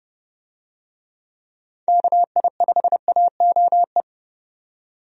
Morse code, heard as a keyed tone, about 31 wpm, 700 Hz. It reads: KI5AOE
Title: USA Call Signs - 6 Characters - 30wpm